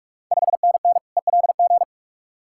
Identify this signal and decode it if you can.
Morse code, heard as a keyed tone, about 41 wpm, 690 Hz. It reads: 5NN ELG
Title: Ontario QSO Party Exchange 45wpm